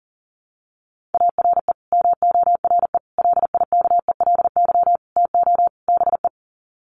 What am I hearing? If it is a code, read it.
Morse code, heard as a keyed tone, about 40 wpm, 700 Hz. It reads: ARE MORE LIKELY TO BE
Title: Sets of 5 Words 40wpm